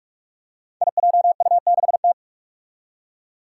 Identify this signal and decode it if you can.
Morse code, heard as a keyed tone, about 42 wpm, 700 Hz. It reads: IJU6T
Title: Five Challenging Letter-Number Combinations 45wpm